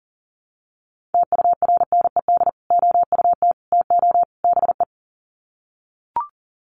Morse code, heard as a keyed tone, about 40 wpm, 700 Hz. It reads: TURNED OUT TO BE
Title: Sets of 4 Words 40wpm